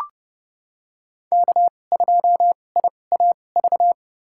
Morse code, heard as a keyed tone, about 31 wpm, spaced as for 22 wpm, 700 Hz. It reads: K2IAV
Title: USA Call Signs - 5 Characters - 2x Character Spacing - 30wpm